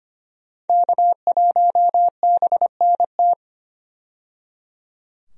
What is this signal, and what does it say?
Morse code, heard as a keyed tone, about 25 wpm, 700 Hz. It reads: K1BNT
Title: USA Call Signs - 5 Characters - 25wpm